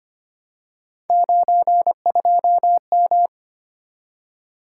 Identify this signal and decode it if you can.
Morse code, heard as a keyed tone, about 25 wpm, 700 Hz. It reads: 92M
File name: Three Letter-Number Combinations - 25wpm